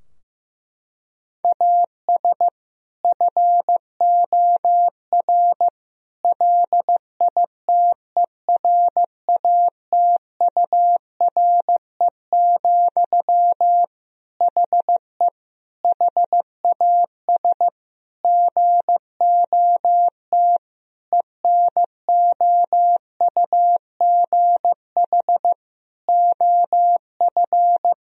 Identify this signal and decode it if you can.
Morse code, heard as a keyed tone, about 15 wpm, 700 Hz. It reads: AS FOR LITERATURE, HE HAS GOT ENOUGH OF
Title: Sentences from Top 1000 Words - Review Part 3 - 15wpm